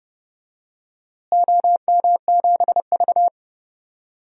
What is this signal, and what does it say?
Morse code, heard as a keyed tone, about 32 wpm, 700 Hz. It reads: OM7V
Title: DXCC-100 Call Signs - 4 Characters - Repeated 2x - 30wpm